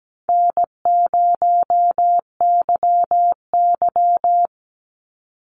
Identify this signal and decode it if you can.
Morse code, heard as a keyed tone, about 17 wpm, 700 Hz. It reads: N0YY
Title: Easy Call Signs Repeated 3x - 17wpm